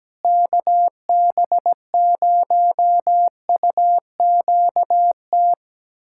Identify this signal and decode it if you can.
Morse code, heard as a keyed tone, about 17 wpm, 700 Hz. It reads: KB0UQT
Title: USA Callsign DE Callsign 17wpm